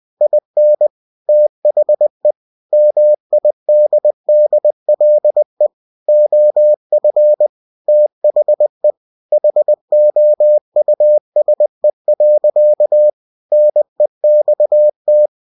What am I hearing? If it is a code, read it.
Morse code, heard as a keyed tone, about 20 wpm, 600 Hz. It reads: IN THE MIDDLE OF THE HOUSE. NEXT
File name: cw-books / winnie-the-pooh / winnie-20wpm-page-0020